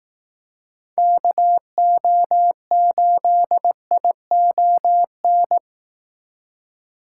Morse code, heard as a keyed tone, about 18 wpm, 700 Hz. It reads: KO8ION
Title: Speed Racing - USA 2x3 Call Signs 15wpm